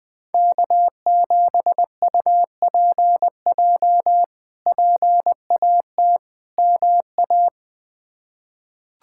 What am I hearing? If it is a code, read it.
Morse code, heard as a keyed tone, about 21 wpm, 700 Hz. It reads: K7UPJ PAT MA
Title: K1USN SST QSO Practice - 20wpm